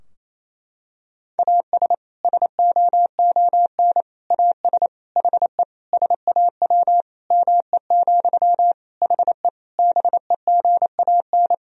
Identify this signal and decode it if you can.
Morse code, heard as a keyed tone, about 27 wpm, 700 Hz. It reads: AS SOON AS HE SAW ME, HE BEGAN
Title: Sentences from Top 1000 Words - Review Part 2 - 28wpm